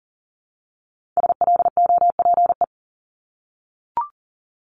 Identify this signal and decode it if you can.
Morse code, heard as a keyed tone, about 38 wpm, 700 Hz. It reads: SLOPE
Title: Multi-letter Phonemes Repeated 3x 40wpm